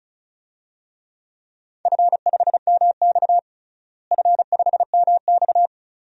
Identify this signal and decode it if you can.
Morse code, heard as a keyed tone, about 34 wpm, 700 Hz. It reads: F5MX F5MX
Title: DXCC-100 Call Signs - 4 Characters - Repeated 3x - 35wpm